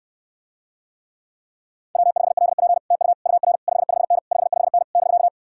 Morse code, heard as a keyed tone, about 69 wpm, 700 Hz. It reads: W5LP TU UR 55N 55N <BK>